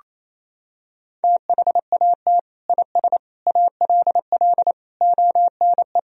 Morse code, heard as a keyed tone, about 28 wpm, 700 Hz. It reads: THAT IS ALL ONE
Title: Sentences from Top 200 Words and Abbreviations 28wpm